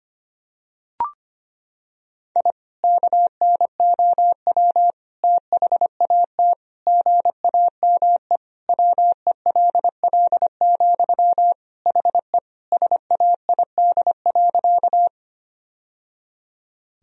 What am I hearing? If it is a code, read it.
Morse code, heard as a keyed tone, about 25 wpm, 700 Hz. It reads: I KNOW THAT GAME WELL, HE SAID.
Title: Sentences from Top 1000 Words - Review Part 3 - 25wpm